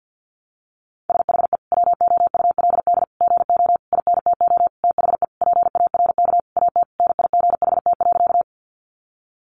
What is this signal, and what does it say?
Morse code, heard as a keyed tone, about 50 wpm, 700 Hz. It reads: SHE WOULD GO INTO THE PARK AT NIGHT.
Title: Sentences from Top 1300 Words 50wpm